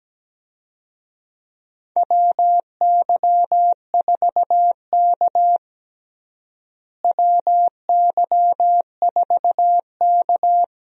Morse code, heard as a keyed tone, about 17 wpm, 700 Hz. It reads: WY4K WY4K
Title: USA Call Signs - 4 Characters - Repeated 2x - 17wpm